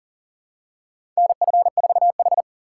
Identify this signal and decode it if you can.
Morse code, heard as a keyed tone, about 40 wpm, 700 Hz. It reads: NF4H
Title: USA Call Signs - 3x Character Spacing - 40wpm